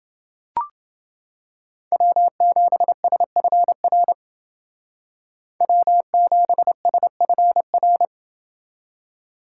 Morse code, heard as a keyed tone, about 28 wpm, 700 Hz. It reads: W7SFR W7SFR